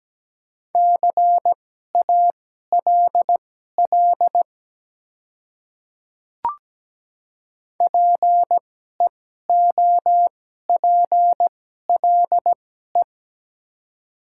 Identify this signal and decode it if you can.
Morse code, heard as a keyed tone, about 17 wpm, spaced as for 12 wpm, 700 Hz. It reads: CALL PEOPLE
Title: 2x Character Spacing - Top 100 Words - 17wpm